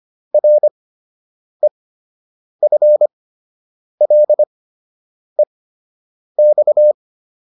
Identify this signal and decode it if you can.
Morse code, heard as a keyed tone, about 25 wpm, spaced as for 8 wpm, 600 Hz. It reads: REFLEX